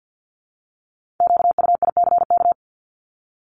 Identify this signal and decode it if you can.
Morse code, heard as a keyed tone, about 50 wpm, 700 Hz. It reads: QUICK
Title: Multi-letter Phonemes 50wpm